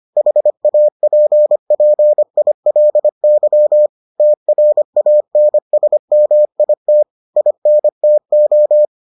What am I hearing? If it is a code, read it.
Morse code, heard as a keyed tone, about 25 wpm, 600 Hz. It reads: HAPPILY TRANSMIT INTO